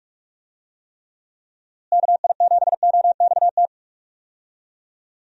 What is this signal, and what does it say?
Morse code, heard as a keyed tone, about 46 wpm, 700 Hz. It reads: KI7OXT
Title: USA Call Signs - 6 Characters - Repeated 2x - 45wpm